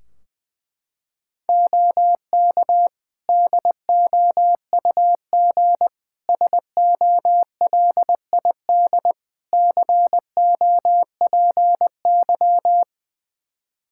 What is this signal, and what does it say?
Morse code, heard as a keyed tone, about 20 wpm, 700 Hz. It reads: OK DOUG SOLID COPY
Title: K1USN SST QSO Practice 20wpm